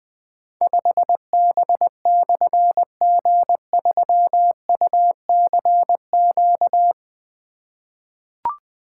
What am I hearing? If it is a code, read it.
Morse code, heard as a keyed tone, about 20 wpm, 700 Hz. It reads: 5B/G3UCQ
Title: Difficult Call Signs 20wpm